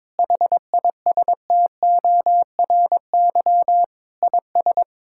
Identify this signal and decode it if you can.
Morse code, heard as a keyed tone, about 22 wpm, 700 Hz. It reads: HISTORY IS
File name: practice / 22wpm / Sentences with 8-Word Phrases - 22wpm